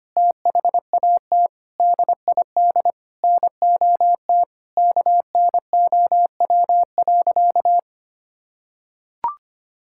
Morse code, heard as a keyed tone, about 25 wpm, 700 Hz. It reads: THAT DID NOT KNOW.
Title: Sentences from Top 1600 Words 25wpm